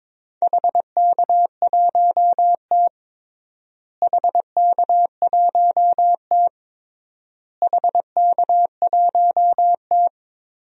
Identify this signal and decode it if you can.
Morse code, heard as a keyed tone, about 22 wpm, 700 Hz. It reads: HK1T HK1T HK1T
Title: DXCC-100 Call Signs - 4 Characters - Repeated 3x - 22wpm